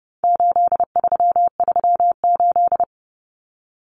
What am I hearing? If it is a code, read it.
Morse code, heard as a keyed tone, about 30 wpm, 700 Hz. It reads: 8338
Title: Numbers - 4-Digit 30wpm